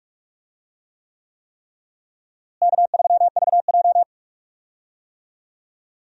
Morse code, heard as a keyed tone, about 45 wpm, 700 Hz. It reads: K3VJ
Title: USA Call Signs - 4 Characters - Repeated 3x - 45wpm